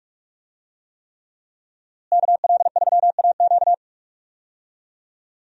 Morse code, heard as a keyed tone, about 42 wpm, 700 Hz. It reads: KL3AQ
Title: USA Call Signs - 5 Characters - Repeated 3x - 45wpm